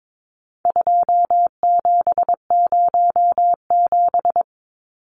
Morse code, heard as a keyed tone, about 22 wpm, 700 Hz. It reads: 2707
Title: Numbers - 4-Digit 22wpm